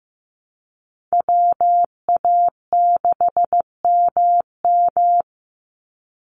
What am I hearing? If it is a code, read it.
Morse code, heard as a keyed tone, about 15 wpm, 700 Hz. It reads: WA6MM
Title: NA SOTA Activator Callsigns 15wpm